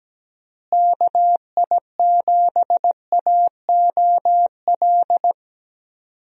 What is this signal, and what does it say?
Morse code, heard as a keyed tone, about 17 wpm, 700 Hz. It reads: KI7AOL
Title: USA Call Signs - 6 Characters - Repeated 3x - 17wpm